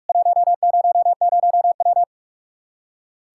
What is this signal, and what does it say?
Morse code, heard as a keyed tone, about 41 wpm, 700 Hz. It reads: 100W
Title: Practice QSO - Exchange 2 - Info - 45wpm